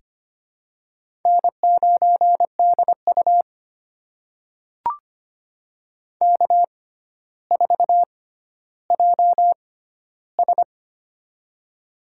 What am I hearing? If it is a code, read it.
Morse code, heard as a keyed tone, about 25 wpm, spaced as for 13 wpm, 700 Hz. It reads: N9DU K4JS
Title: USA Call Signs - 4 Characters - 6x Character Spacing - 25wpm